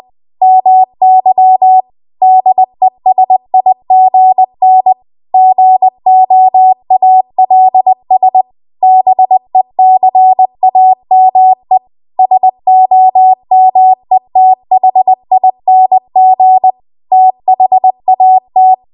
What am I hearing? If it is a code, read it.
Morse code, heard as a keyed tone, about 20 wpm, 750 Hz. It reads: MY DESIGN GOALS BECAME SOMETHING THAT
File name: Archive / 20 WPM / 20WPM